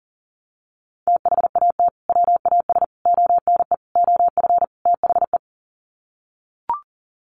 Morse code, heard as a keyed tone, about 40 wpm, 700 Hz. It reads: THAT WAS ONE OF THE
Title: Sets of 5 Words 40wpm